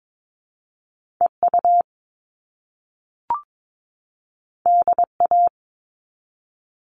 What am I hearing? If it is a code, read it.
Morse code, heard as a keyed tone, about 22 wpm, 700 Hz. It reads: EU DA
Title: All Cut Numbers - 2-Digit 22wpm